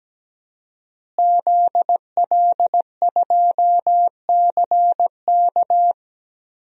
Morse code, heard as a keyed tone, about 17 wpm, 700 Hz. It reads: ZL2CK
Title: DXCC-100 Call Signs - 5 Characters - Repeated 3x - 17wpm